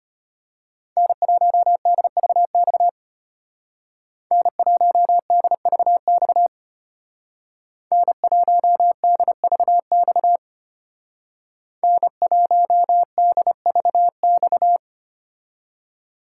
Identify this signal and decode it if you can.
Morse code, heard as a keyed tone, about 31 wpm, 700 Hz. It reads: N1DVX N1DVX N1DVX N1DVX